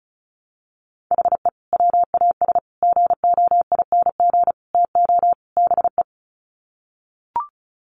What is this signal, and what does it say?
Morse code, heard as a keyed tone, about 34 wpm, 700 Hz. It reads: HE WAS GOING TO BE
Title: Sets of 5 Words 35wpm